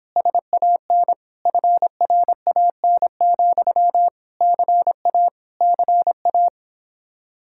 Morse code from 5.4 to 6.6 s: CA

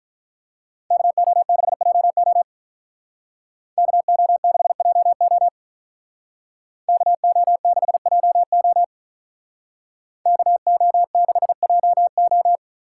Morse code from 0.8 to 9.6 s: KO6JO KO6JO KO6JO

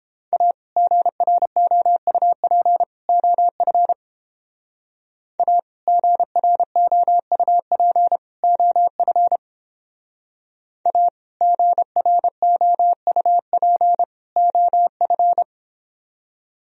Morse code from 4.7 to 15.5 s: A GROUP OF A GROUP OF

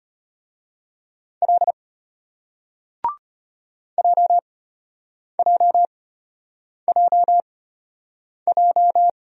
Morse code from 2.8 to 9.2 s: JJJJ

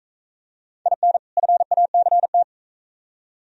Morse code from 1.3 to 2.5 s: FACT

